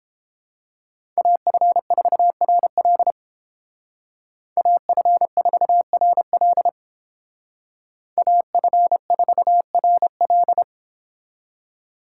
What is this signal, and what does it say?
Morse code, heard as a keyed tone, about 29 wpm, 700 Hz. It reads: AF4RL AF4RL AF4RL